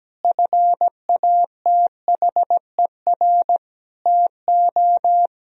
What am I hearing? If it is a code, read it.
Morse code, heard as a keyed tone, about 17 wpm, 700 Hz. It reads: FATHER TO
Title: Sentences from Top 1000 Words - Review Part 1 - 17wpm